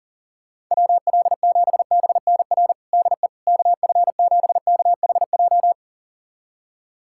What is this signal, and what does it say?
Morse code, heard as a keyed tone, about 40 wpm, 700 Hz. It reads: WL7BNR DE KF7KHJ
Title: USA Callsign DE Callsign - 6 Characters - 40wpm